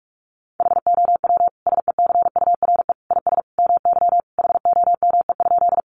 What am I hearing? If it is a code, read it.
Morse code, heard as a keyed tone, about 44 wpm, 700 Hz. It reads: HOW SECURE IS MY HOME?
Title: Sentences from Top 1200 Words 45wpm